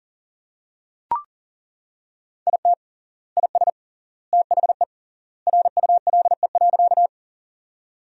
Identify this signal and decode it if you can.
Morse code, heard as a keyed tone, about 40 wpm, 700 Hz. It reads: IT IS THE RULE.
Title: 3x Word-Spacing - Sentences from Top 400 Words - 40wpm